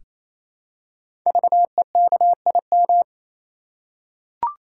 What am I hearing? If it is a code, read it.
Morse code, heard as a keyed tone, about 27 wpm, 700 Hz. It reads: VEKIM
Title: Code Groups - 5 Letters - 28wpm